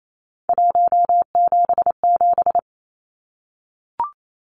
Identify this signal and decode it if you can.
Morse code, heard as a keyed tone, about 28 wpm, 700 Hz. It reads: 177
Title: Numbers - 3-Digit 28wpm